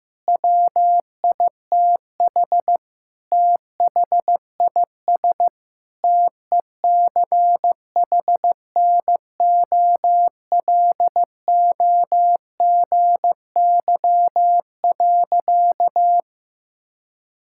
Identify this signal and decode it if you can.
Morse code, heard as a keyed tone, about 15 wpm, 700 Hz. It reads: WITH THIS TECHNOLOGY.
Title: Sentences from Top 1000 Words - Review Part 5 - 15wpm